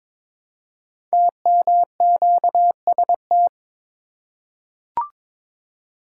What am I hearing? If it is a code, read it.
Morse code, heard as a keyed tone, about 22 wpm, 700 Hz. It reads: TMQST
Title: Code Groups - 5 Letter:Numbers - 22wpm